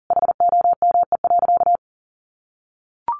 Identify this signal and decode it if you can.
Morse code, heard as a keyed tone, about 40 wpm, 700 Hz. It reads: HOME.
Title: Sentences from Top 500 Words 40wpm